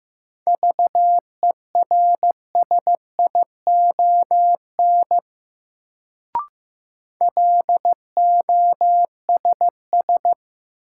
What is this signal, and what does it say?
Morse code, heard as a keyed tone, about 15 wpm, 700 Hz. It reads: VERSION LOSS
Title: Sentences from Top 1400 Words - New Word Drill - 15wpm